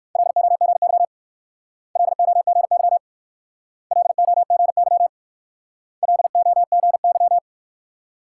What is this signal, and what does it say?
Morse code, heard as a keyed tone, about 56 wpm, 690 Hz. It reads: LOGY LOGY LOGY LOGY
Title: Speed-Racing Multi-letter Phonemes - 45wpm